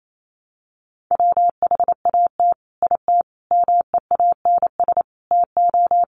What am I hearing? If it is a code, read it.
Morse code, heard as a keyed tone, about 28 wpm, 700 Hz. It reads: WHAT IT MEANS TO
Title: Sets of 5 Words 28wpm